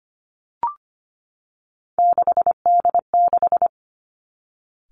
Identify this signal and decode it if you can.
Morse code, heard as a keyed tone, about 25 wpm, 700 Hz. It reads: 6D6
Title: Three Challenging Letter-Number Combinations 25wpm